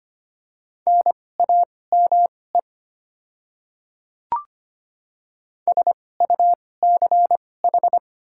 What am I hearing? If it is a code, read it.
Morse code, heard as a keyed tone, about 25 wpm, spaced as for 18 wpm, 700 Hz. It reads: NAME SUCH
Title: 2x Character Spacing - Top 300 Words - 25wpm